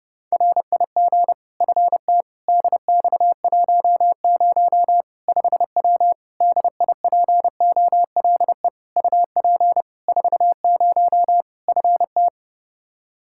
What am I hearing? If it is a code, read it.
Morse code, heard as a keyed tone, about 30 wpm, 700 Hz. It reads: RIG FT DX10 5W DIPOLE UP 40 FT
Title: Practice QSO - Exchange 2 - Info - 30wpm